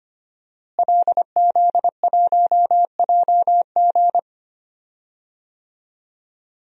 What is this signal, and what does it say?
Morse code, heard as a keyed tone, about 25 wpm, 700 Hz. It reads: LZ1JG
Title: DXCC-100 Call Signs - 5 Characters - Repeated 2x - 25wpm